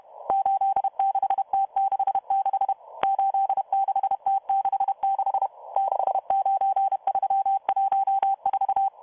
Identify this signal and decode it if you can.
Morse code, heard as a keyed tone, about 31 wpm, 780 Hz. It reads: 86T66 86T66 69314